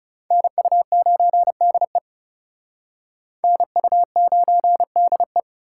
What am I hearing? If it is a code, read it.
Morse code, heard as a keyed tone, about 37 wpm, 700 Hz. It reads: NU9DE NU9DE
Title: Speed Racing - USA 2x2 Call Signs 30wpm